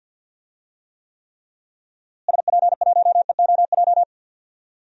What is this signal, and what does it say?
Morse code, heard as a keyed tone, about 48 wpm, 700 Hz. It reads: SP1EOJ
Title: DXCC-100 Call Signs - 6 Characters - Repeated 3x - 50wpm